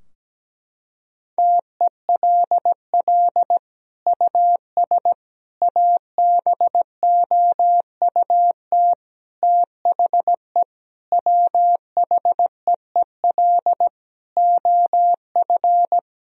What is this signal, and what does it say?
Morse code, heard as a keyed tone, about 17 wpm, 700 Hz. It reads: TELL US ABOUT THE WHEEL OF
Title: Sentences from Top 1000 Words - Review Part 3 - 17wpm